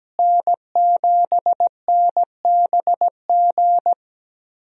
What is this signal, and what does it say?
Morse code, heard as a keyed tone, about 17 wpm, 700 Hz. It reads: N7NBG
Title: USA Call Signs - 5 Characters - Repeated 2x - 17wpm